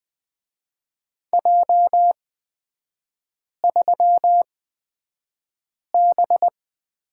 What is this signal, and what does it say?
Morse code, heard as a keyed tone, about 20 wpm, spaced as for 5 wpm, 700 Hz. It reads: J3B
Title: 229 Lesson 37e - Letter x - Three Characters 20wpm